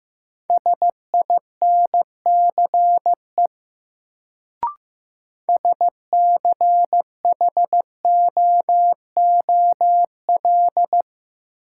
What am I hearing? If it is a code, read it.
Morse code, heard as a keyed tone, about 15 wpm, 700 Hz. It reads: SINCE SCHOOL
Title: Sentences from Top 300 Words - New Word Drill - 15wpm